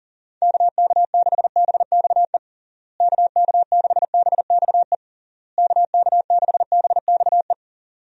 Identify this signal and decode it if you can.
Morse code, heard as a keyed tone, about 40 wpm, 700 Hz. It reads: KK6BXE KK6BXE KK6BXE
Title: USA Call Signs - 6 Characters - Repeated 3x - 40wpm